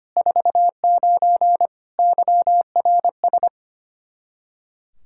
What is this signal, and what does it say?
Morse code, heard as a keyed tone, about 25 wpm, 700 Hz. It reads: 49 YRS
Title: K1USN SST QSO Practice 25wpm